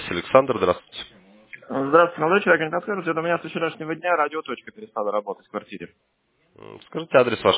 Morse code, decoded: none

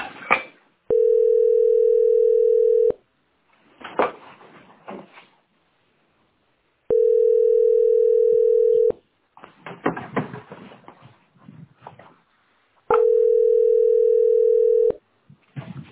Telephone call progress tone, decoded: ringback tone

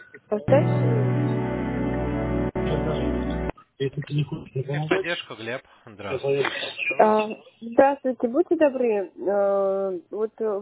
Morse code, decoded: none

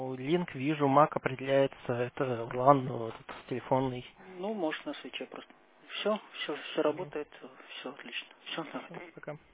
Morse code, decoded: none

